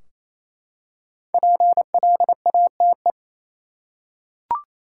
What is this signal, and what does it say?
Morse code, heard as a keyed tone, about 28 wpm, 700 Hz. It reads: PLATE
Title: Five Letter Words - 28wpm